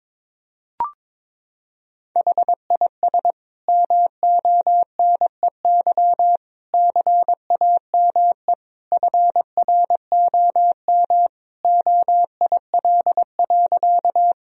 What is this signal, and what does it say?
Morse code, heard as a keyed tone, about 22 wpm, 700 Hz. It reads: HIS MONEY CAME FROM OIL.
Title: Sentences from Top 1000 Words - Review Part 4 - 22wpm